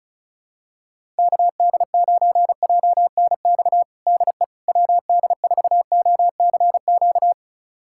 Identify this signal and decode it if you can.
Morse code, heard as a keyed tone, about 36 wpm, 700 Hz. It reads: KD9JNX DE WD4OCQ
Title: USA Callsign DE Callsign 35wpm